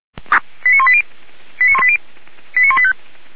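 Five-tone selective call (ZVEI), heard as quick runs of five tones, tones of about 70 ms. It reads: 89180 89180 89185
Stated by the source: SDR-received ZVEI five-tone selective call, code 89131.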